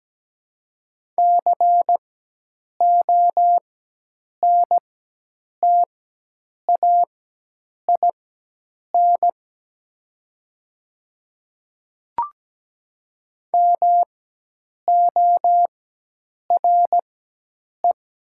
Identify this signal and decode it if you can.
Morse code, heard as a keyed tone, about 17 wpm, spaced as for 8 wpm, 700 Hz. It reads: CONTAIN MORE
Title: 4x Character Spacing - Top 500 Words - 17wpm